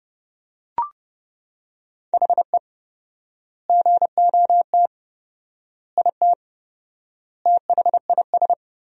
Morse code, heard as a keyed tone, about 32 wpm, 700 Hz. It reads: HE GOT IT THIS
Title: 4x Word-Spacing - Sentences from Top 400 Words - 30wpm